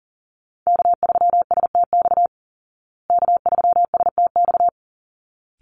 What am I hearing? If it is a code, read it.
Morse code, heard as a keyed tone, about 40 wpm, 700 Hz. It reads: K3STX K3STX
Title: Call Signs Repeated 3x - 40wpm